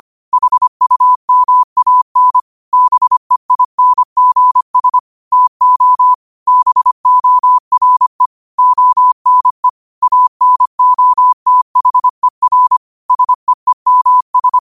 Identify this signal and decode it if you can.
Morse code, heard as a keyed tone, about 25 wpm, 1000 Hz. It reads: HUMAN BEINGS TO BORE ONE ANOTHER SEEMS
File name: qotdfiles / 25wpm